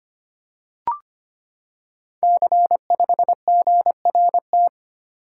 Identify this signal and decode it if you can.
Morse code, heard as a keyed tone, about 25 wpm, 700 Hz. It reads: C5GRT
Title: Code Groups - 5 Letter:Numbers - 25wpm